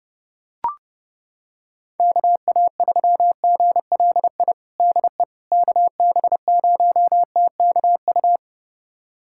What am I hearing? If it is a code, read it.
Morse code, heard as a keyed tone, about 30 wpm, 700 Hz. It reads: KA3GLI DE KB0TKU
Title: USA Callsign DE Callsign - 6 Characters - 30wpm